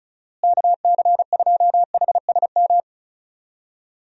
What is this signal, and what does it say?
Morse code, heard as a keyed tone, about 34 wpm, 700 Hz. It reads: KC2HSM
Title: USA Call Signs - 6 Characters - Repeated 2x - 35wpm